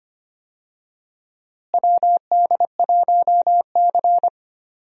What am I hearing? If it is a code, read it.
Morse code, heard as a keyed tone, about 25 wpm, 700 Hz. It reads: WD1C
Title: USA Call Signs - 4 Characters - Repeated 2x - 25wpm